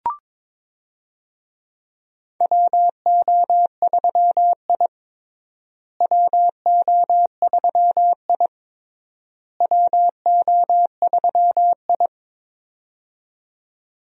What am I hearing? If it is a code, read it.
Morse code, heard as a keyed tone, about 22 wpm, 700 Hz. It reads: WO3I WO3I WO3I
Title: USA Call Signs - 4 Characters - Repeated 3x - 22wpm